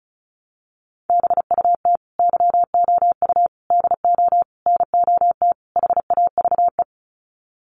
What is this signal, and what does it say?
Morse code, heard as a keyed tone, about 34 wpm, 700 Hz. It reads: BUT YOU DO NOT HAVE
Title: Sets of 5 Words Encore 35wpm